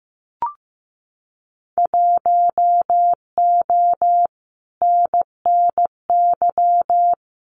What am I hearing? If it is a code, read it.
Morse code, heard as a keyed tone, about 15 wpm, 700 Hz. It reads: 1O NNY
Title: Winter Field Day Exchanges 15wpm